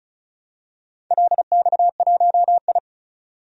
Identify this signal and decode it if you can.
Morse code, heard as a keyed tone, about 36 wpm, 690 Hz. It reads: LX1I
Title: DXCC-100 Call Signs - 4 Characters - 35wpm